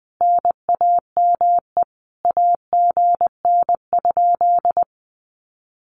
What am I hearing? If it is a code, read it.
Morse code, heard as a keyed tone, about 20 wpm, 700 Hz. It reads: NAME AGN?
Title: Common QSO Phrases Encore 20wpm